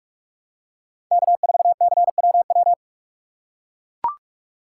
Morse code, heard as a keyed tone, about 44 wpm, 700 Hz. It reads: K4CWW
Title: USA Call Signs - 5 Characters - 3x Character Spacing - 45wpm